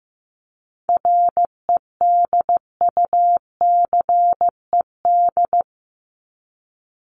Morse code, heard as a keyed tone, about 15 wpm, 700 Hz. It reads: REDUCED
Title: ED Suffix - Top 100 Words with Base Words 15wpm